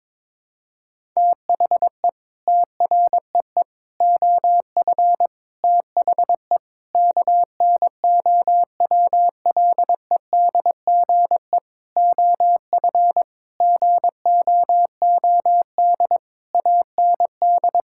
Morse code, heard as a keyed tone, about 22 wpm, 700 Hz. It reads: THE TREE OF THE KNOWLEDGE OF GOOD AND
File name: Sets of 9 Words - 22wpm